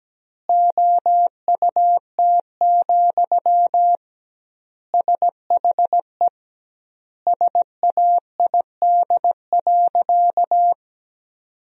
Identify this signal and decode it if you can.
Morse code, heard as a keyed tone, about 17 wpm, 700 Hz. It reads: OUT, SHE SAID.
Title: 2x Word-Spacing - Sentences from Top 500 Words - 17wpm